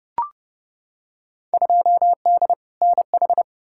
Morse code, heard as a keyed tone, about 30 wpm, 700 Hz. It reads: 2D NH
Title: Field Day Exchanges Encore - 30wpm